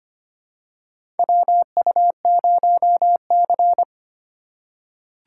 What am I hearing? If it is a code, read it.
Morse code, heard as a keyed tone, about 25 wpm, 700 Hz. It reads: WU0C
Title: USA Call Signs - 4 Characters - 3x Character Spacing - 25wpm